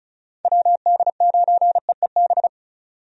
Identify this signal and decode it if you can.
Morse code, heard as a keyed tone, about 34 wpm, 700 Hz. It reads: WD9EEB
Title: USA Call Signs - 6 Characters - 35wpm